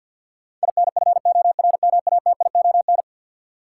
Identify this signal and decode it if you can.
Morse code, heard as a keyed tone, about 52 wpm, 700 Hz. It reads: INFORMATION
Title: Sentences from Top 600 Words - New Word Drill - 50wpm